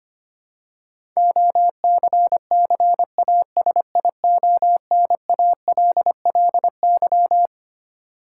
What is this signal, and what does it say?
Morse code, heard as a keyed tone, about 25 wpm, 700 Hz. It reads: OCCASIONALLY
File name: Twelve Letter Words - 25wpm